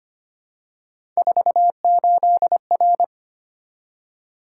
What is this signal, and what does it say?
Morse code, heard as a keyed tone, about 25 wpm, 700 Hz. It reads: 48R
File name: Three Letter-Number Combinations - 25wpm